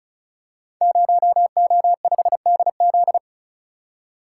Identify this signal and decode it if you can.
Morse code, heard as a keyed tone, about 36 wpm, 700 Hz. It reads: 0O5DZ
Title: Code Groups - 5 Letter:Numbers - 35wpm